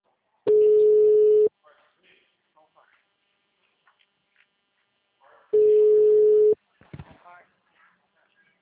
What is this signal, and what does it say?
Ringback tone (Europe, 425 Hz), with speech.